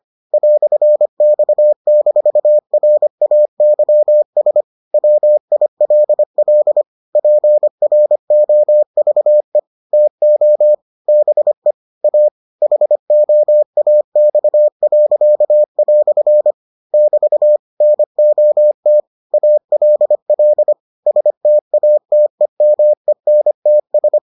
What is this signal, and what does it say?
Morse code, heard as a keyed tone, about 25 wpm, 600 Hz. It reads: "X-RAYS WILL PROVE TO BE A HOAX." <BT> NOT ALL STATEMENTS